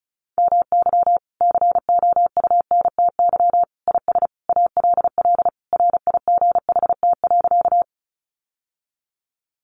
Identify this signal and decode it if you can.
Morse code, heard as a keyed tone, about 34 wpm, 700 Hz. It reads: MY COUNTY IS ALL RIGHT.
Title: Sentences from Top 1200 Words 35wpm